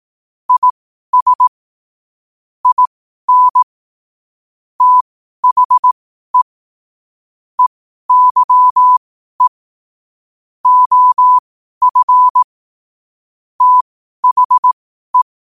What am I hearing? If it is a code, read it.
Morse code, heard as a keyed tone, about 18 wpm, spaced as for 12 wpm, 1000 Hz. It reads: IS IN THE EYE OF THE
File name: qotdfiles / 12wpm